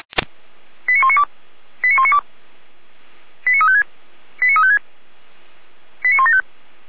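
Five-tone selective call (ZVEI), heard as quick runs of five tones, tones of about 70 ms. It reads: 89192 89192 89356 89356 89175
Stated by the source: ZVEI five-tone selective call received on SDR.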